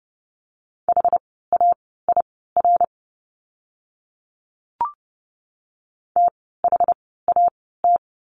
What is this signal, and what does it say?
Morse code, heard as a keyed tone, about 30 wpm, spaced as for 17 wpm, 700 Hz. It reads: HAIR THAT
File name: Lesson 17f - Letter u - Words 30wpm